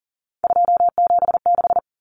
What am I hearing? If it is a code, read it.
Morse code, heard as a keyed tone, about 40 wpm, 700 Hz. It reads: 276
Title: Numbers - 3-Digit 40wpm